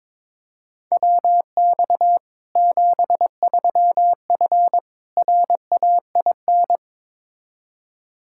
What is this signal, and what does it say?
Morse code, heard as a keyed tone, about 21 wpm, 700 Hz. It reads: WX 73F RAIN
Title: Practice QSO - Exchange 2 - Info - 22wpm